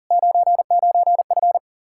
Morse code, heard as a keyed tone, about 47 wpm, 700 Hz. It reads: 99F
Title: Speed-Racing Common QSO Phrases - 40wpm